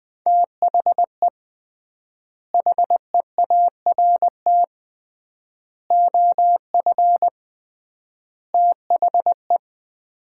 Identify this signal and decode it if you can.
Morse code, heard as a keyed tone, about 20 wpm, 700 Hz. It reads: THE HEART OF THE
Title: 3x Word-Spacing - Sets of 4 Words - 20wpm